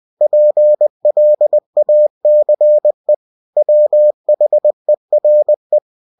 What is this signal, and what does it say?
Morse code, heard as a keyed tone, about 20 wpm, 600 Hz. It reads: PLACE WHERE
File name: cw-books / winnie-the-pooh / winnie-20wpm-page-0083